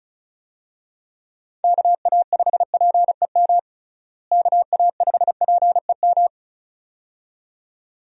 Morse code, heard as a keyed tone, about 34 wpm, 700 Hz. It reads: KA5PEM KA5PEM
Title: USA Call Signs - 6 Characters - Repeated 2x - 35wpm